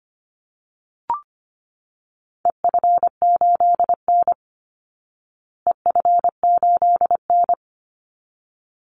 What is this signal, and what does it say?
Morse code, heard as a keyed tone, about 25 wpm, 700 Hz. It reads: EF8N EF8N
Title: Call Signs Repeated 3x - 25wpm